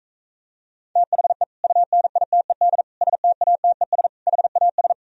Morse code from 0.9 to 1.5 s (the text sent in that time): THE